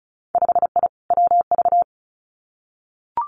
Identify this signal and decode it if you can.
Morse code, heard as a keyed tone, about 36 wpm, 700 Hz. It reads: 5I WV
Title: Winter Field Day Exchanges Encore 35wpm